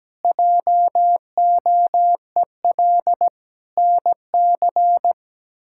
Morse code, heard as a keyed tone, about 17 wpm, 700 Hz. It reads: JOEL NC
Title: K1USN SST QSO Practice - 17wpm